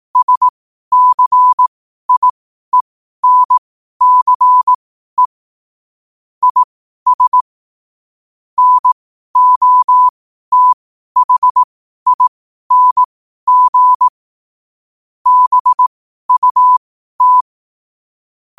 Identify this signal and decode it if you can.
Morse code, heard as a keyed tone, about 18 wpm, spaced as for 12 wpm, 1000 Hz. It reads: SCIENCE IS NOTHING BUT
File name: qotdfiles / 12wpm